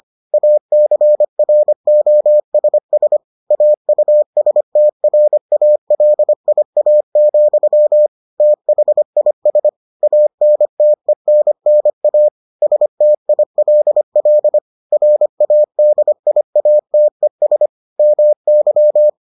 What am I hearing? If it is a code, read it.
Morse code, heard as a keyed tone, about 25 wpm, 600 Hz. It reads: ACROSS AUSTRALIA, THIS ANTENNA STILL RADIATES MY